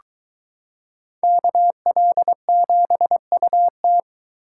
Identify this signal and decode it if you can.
Morse code, heard as a keyed tone, about 22 wpm, 700 Hz. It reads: KL7UT